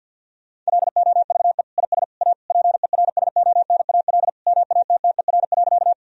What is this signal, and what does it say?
Morse code, heard as a keyed tone, about 50 wpm, 700 Hz. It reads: LOVE IS A PERSONAL MATTER.
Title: Sentences from Top 1000 Words - Review Part 4 - 50wpm